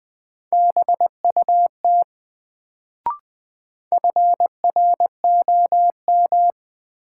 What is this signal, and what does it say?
Morse code, heard as a keyed tone, about 20 wpm, 700 Hz. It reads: BUT FROM
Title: Sets of 2 Words - New Word Drill - 20wpm